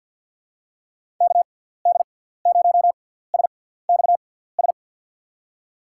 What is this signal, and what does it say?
Morse code, heard as a keyed tone, about 46 wpm, spaced as for 17 wpm, 700 Hz. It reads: KD0SXS
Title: USA Call Signs - 6 Characters - 6x Character Spacing - 50wpm